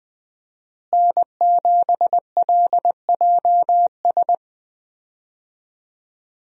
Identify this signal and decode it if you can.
Morse code, heard as a keyed tone, about 20 wpm, 700 Hz. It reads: N7LJS
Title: USA Call Signs - 5 Characters - Repeated 2x - 20wpm